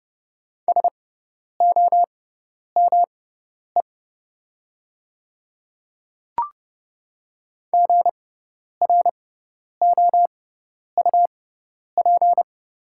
Morse code, heard as a keyed tone, about 32 wpm, spaced as for 10 wpm, 700 Hz. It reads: SOME GROUP
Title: 6x Character Spacing - Top 500 Words - 30wpm